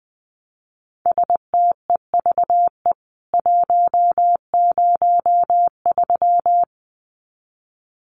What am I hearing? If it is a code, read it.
Morse code, heard as a keyed tone, about 20 wpm, 700 Hz. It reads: STEVE 103
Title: CWT Exchange 20wpm